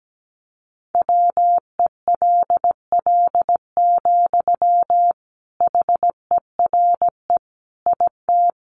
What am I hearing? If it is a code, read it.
Morse code, heard as a keyed tone, about 17 wpm, 700 Hz. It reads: WELL, HERE IT
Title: Sentences from Top 1400 Words 17wpm